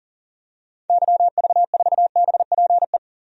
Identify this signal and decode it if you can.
Morse code, heard as a keyed tone, about 40 wpm, 700 Hz. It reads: YV4BPE
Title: DXCC-100 Call Signs - 6 Characters - 40wpm